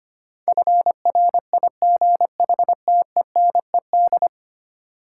Morse code, heard as a keyed tone, about 25 wpm, 700 Hz. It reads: FRIGHTENED
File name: practice / 25wpm / Ten Letter Words - 25wpm